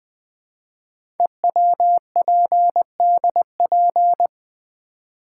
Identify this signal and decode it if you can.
Morse code, heard as a keyed tone, about 20 wpm, 700 Hz. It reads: EWPDP
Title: Code Groups - 5 Letters - 20wpm